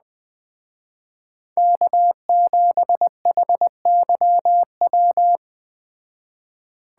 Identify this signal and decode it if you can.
Morse code, heard as a keyed tone, about 20 wpm, 700 Hz. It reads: K7HYW
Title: Speed Racing - USA Call Signs 17wpm